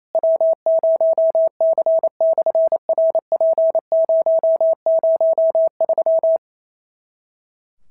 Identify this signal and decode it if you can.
Morse code, heard as a keyed tone, about 28 wpm, 650 Hz. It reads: W0C/RP003
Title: SOTA QSO Practice 28wpm